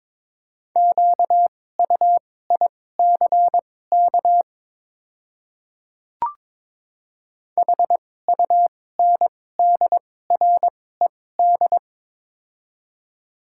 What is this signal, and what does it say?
Morse code, heard as a keyed tone, about 22 wpm, spaced as for 16 wpm, 690 Hz. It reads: QUICK HUNDRED
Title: 2x Character Spacing - Top 500 Words - 22wpm